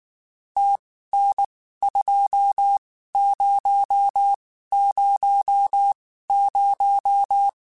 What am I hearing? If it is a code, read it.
Morse code, heard as a keyed tone, about 19 wpm, spaced as for 14 wpm, 800 Hz. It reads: TN2000
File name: QSOs Spaced